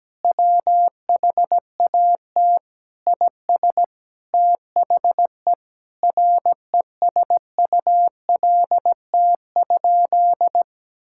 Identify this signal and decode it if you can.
Morse code, heard as a keyed tone, about 17 wpm, 700 Hz. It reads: WHAT IS THE RESULT?
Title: Sentences from Top 1600 Words 17wpm